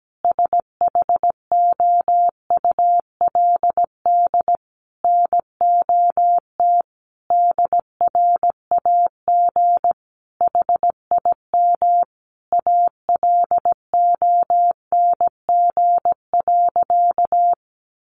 Morse code, heard as a keyed tone, about 17 wpm, 700 Hz. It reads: SHOULD NOT DRAG HIM ALONG.
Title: Sentences from Top 1500 Words 17wpm